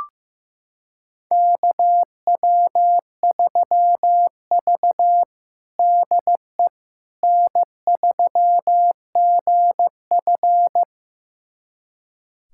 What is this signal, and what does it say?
Morse code, heard as a keyed tone, about 15 wpm, 700 Hz. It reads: KW3V DE N3GF
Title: USA Callsign DE Callsign - 4 Characters - 15wpm